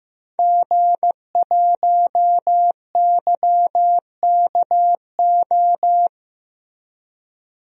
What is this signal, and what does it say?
Morse code, heard as a keyed tone, about 15 wpm, 700 Hz. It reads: G1YKO
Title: DXCC-100 Call Signs - 5 Characters - Repeated 3x - 15wpm